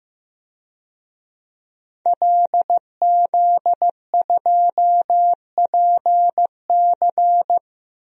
Morse code, heard as a keyed tone, about 15 wpm, 700 Hz. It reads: LZ2PC
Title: DXCC-100 Call Signs - 5 Characters - 15wpm